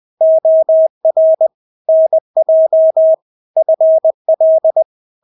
Morse code, heard as a keyed tone, about 20 wpm, 640 Hz. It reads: OR NJ FL
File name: State-Abbreviations-20